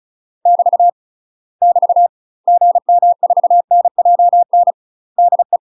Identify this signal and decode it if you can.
Morse code, heard as a keyed tone, about 34 wpm, 700 Hz. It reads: <BT> <BT> GM4NJN DE
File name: UK MASTER QSOs TRAINING / 35WPM / UK MASTER QSOs 0005